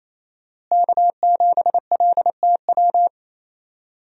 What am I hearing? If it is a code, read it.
Morse code, heard as a keyed tone, about 28 wpm, 700 Hz. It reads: K7LTW
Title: USA Call Signs - 5 Characters - Repeated 2x - 28wpm